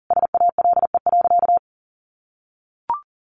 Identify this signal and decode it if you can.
Morse code, heard as a keyed tone, about 40 wpm, 700 Hz. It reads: SALE.
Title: Sentences from Top 1400 Words 40wpm